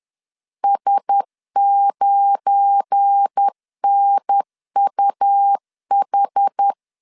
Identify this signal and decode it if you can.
Morse code, heard as a keyed tone, about 10 wpm, 800 Hz. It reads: S9NUH
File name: CW / DET / 1 9 1500C S B 10wpm 4439